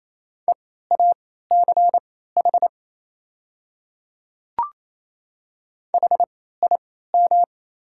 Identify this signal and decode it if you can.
Morse code, heard as a keyed tone, about 28 wpm, spaced as for 16 wpm, 700 Hz. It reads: EACH HIM
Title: 3x Character Spacing - Top 100 Words - 28wpm